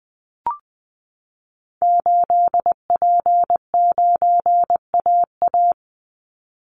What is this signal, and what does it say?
Morse code, heard as a keyed tone, about 20 wpm, 700 Hz. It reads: 8P9AA
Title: Call Signs - 20wpm